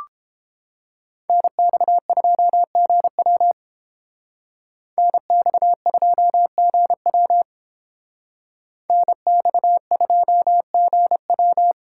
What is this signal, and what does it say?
Morse code, heard as a keyed tone, about 30 wpm, 700 Hz. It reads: NX2GW NX2GW NX2GW